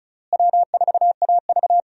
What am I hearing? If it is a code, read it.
Morse code, heard as a keyed tone, about 34 wpm, 700 Hz. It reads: W4AV
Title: USA Call Signs - 4 Characters - Repeated 3x - 35wpm